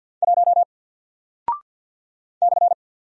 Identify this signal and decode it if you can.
Morse code, heard as a keyed tone, about 48 wpm, 700 Hz. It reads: 1 /
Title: Single Character (Letter:Number:Punctuation:Prosign) 50wpm